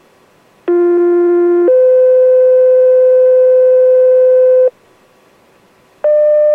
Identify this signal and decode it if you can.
Two-tone page, heard as a one-second tone, then a three-second tone, about 350 then 510 Hz. It is heard from 0.7 to 4.7 s.